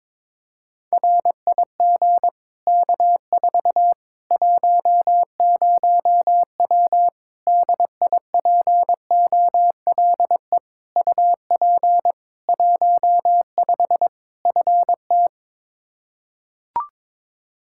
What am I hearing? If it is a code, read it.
Morse code, heard as a keyed tone, about 22 wpm, 700 Hz. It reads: RIG K4 10W DIPOLE UP 15 FT
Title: Practice QSO - Exchange 2 - Info - 22wpm